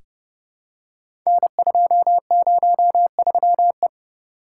Morse code, heard as a keyed tone, about 30 wpm, 700 Hz. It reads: N203E
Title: Code Groups - 5 Letter:Numbers - 30wpm